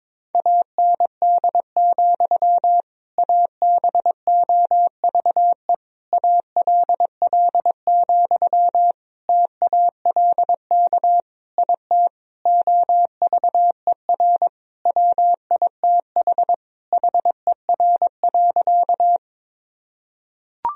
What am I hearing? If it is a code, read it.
Morse code, heard as a keyed tone, about 22 wpm, 700 Hz. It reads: AND, ABOVE ALL, TALK IT OVER WITH HER.
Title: Sentences from Top 1000 Words - Review Part 2 - 22wpm